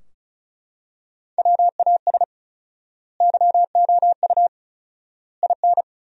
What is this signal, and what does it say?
Morse code, heard as a keyed tone, about 34 wpm, 700 Hz. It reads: WAS YOU IN
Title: 4x Word-Spacing - Sentences from Top 300 Words - 35wpm